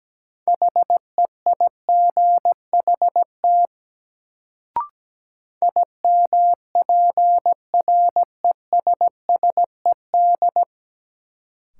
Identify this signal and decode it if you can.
Morse code, heard as a keyed tone, about 17 wpm, 700 Hz. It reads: HEIGHT IMPRESSED
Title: Sentences from Top 1600 Words - New Word Drill - 17wpm